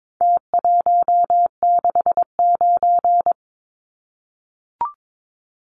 Morse code, heard as a keyed tone, about 22 wpm, 700 Hz. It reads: T169
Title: Numbers - Contest Style - 4-Digit 22wpm